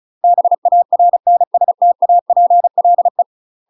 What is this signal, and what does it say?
Morse code, heard as a keyed tone, about 36 wpm, 700 Hz. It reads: BARNSTAPLE
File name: UK MASTER QSOs TRAINING / 35WPM / UK MASTER QSOs 0009